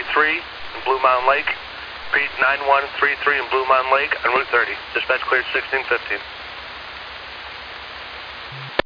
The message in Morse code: none

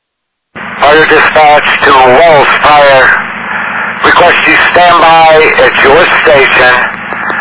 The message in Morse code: none

